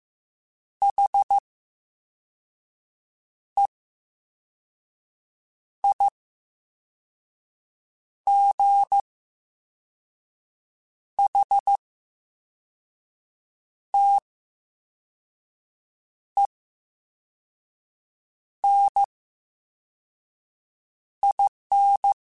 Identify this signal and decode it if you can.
Morse code, heard as a keyed tone, about 15 wpm, spaced as for 4 wpm, 800 Hz. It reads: HEIGHTENIN
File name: Special INGs / Special INGs-mp3s / Slow ing @13